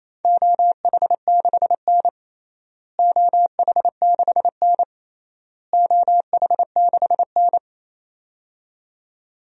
Morse code, heard as a keyed tone, about 28 wpm, 700 Hz. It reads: OH6N OH6N OH6N
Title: DXCC-100 Call Signs - 4 Characters - Repeated 3x - 28wpm